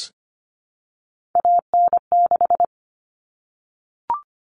Morse code, heard as a keyed tone, about 25 wpm, 700 Hz. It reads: AN6
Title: Common Cut Numbers - 3-Digit 25wpm